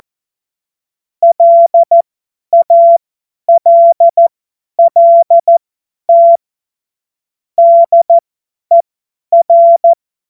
Morse code, heard as a keyed tone, about 14 wpm, spaced as for 10 wpm, 680 Hz. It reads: LALLT DER
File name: Morsekurs / Witze / Witz 01. Juni 2021, 50 F70